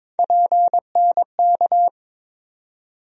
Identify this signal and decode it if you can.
Morse code, heard as a keyed tone, about 22 wpm, 700 Hz. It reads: PNK
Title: USA Call Sign Postfixes - 3 Letters 22wpm